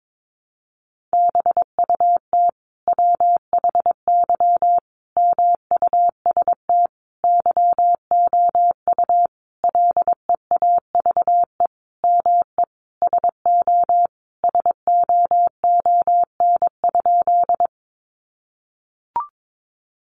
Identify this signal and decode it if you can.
Morse code, heard as a keyed tone, about 22 wpm, 700 Hz. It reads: BUT WHY MUST YOU LEAVE ME SO SOON?
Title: Sentences from Top 400 Words 22wpm